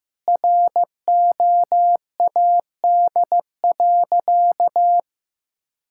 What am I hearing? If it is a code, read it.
Morse code, heard as a keyed tone, about 16 wpm, 700 Hz. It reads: ROAD.
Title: Sentences from Top 1000 Words - Review Part 5 - 15wpm